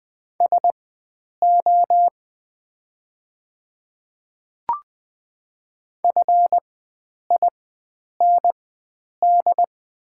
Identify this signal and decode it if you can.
Morse code, heard as a keyed tone, about 20 wpm, spaced as for 9 wpm, 700 Hz. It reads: SO FIND
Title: 4x Character Spacing - Top 100 Words - 20wpm